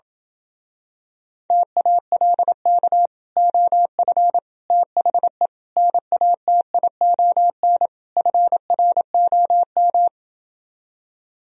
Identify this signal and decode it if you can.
Morse code, heard as a keyed tone, about 27 wpm, 700 Hz. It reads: TALK OF THE NATION FROM